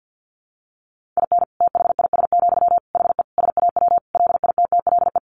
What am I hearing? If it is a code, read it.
Morse code, heard as a keyed tone, about 50 wpm, 700 Hz. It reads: IN THIS, HE SAW LITTLE
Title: Sentences from Top 1500 Words 50wpm